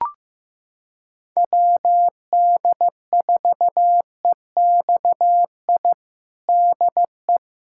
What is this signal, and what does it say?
Morse code, heard as a keyed tone, about 16 wpm, 700 Hz. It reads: WD4EXI DE
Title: USA Callsign DE Callsign - 6 Characters - 15wpm